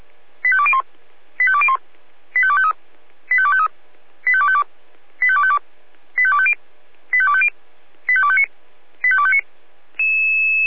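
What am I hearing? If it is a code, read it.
Five-tone selective call (ZVEI), heard as quick runs of five tones, tones of about 70 ms. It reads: 85201 85201 85253 85253 85252 85252 85250 85250 85259 85259
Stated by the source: ZVEI five-tone call received by SDR.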